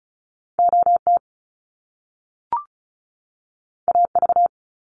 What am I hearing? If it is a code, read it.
Morse code, heard as a keyed tone, about 36 wpm, 700 Hz. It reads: OT AV
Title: Top Two Letter Combinations in Words 35wpm